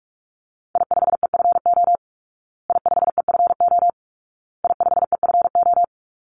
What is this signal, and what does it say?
Morse code, heard as a keyed tone, about 44 wpm, 700 Hz. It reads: I5EFO I5EFO I5EFO
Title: Call Signs Repeated 2x - 45wpm